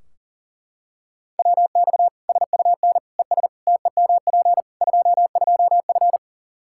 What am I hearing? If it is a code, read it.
Morse code, heard as a keyed tone, about 37 wpm, 700 Hz. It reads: WX SUN ES TEMP 22F
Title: Practice QSO - Exchange 2 - Info - 40wpm